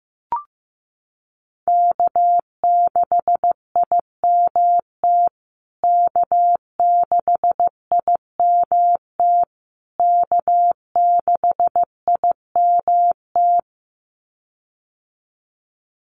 Morse code, heard as a keyed tone, about 15 wpm, 700 Hz. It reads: K6IMT K6IMT K6IMT
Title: USA Call Signs Repeated 3x 15wpm